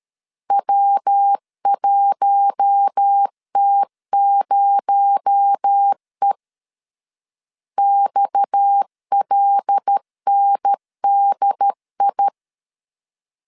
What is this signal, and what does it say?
Morse code, heard as a keyed tone, about 13 wpm, 800 Hz. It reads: W1T0E XLNDI
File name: CW / DET / 1 6 1500C S A 12wpm 3440